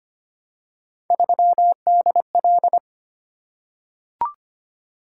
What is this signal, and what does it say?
Morse code, heard as a keyed tone, about 25 wpm, 700 Hz. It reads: 3DL
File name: Three Letter-Number Combinations - 25wpm